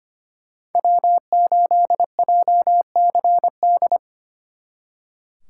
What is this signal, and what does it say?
Morse code, heard as a keyed tone, about 33 wpm, 700 Hz. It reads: W8JCD